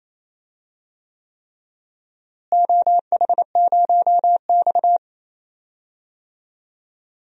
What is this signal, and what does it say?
Morse code, heard as a keyed tone, about 28 wpm, 700 Hz. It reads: OH0X
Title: DXCC-100 Call Signs - 4 Characters - Repeated 3x - 28wpm